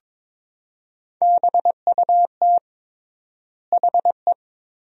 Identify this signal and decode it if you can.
Morse code, heard as a keyed tone, about 22 wpm, 700 Hz. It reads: BUT HE
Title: 3x Word-Spacing - Sentences from Top 100 Words - 22wpm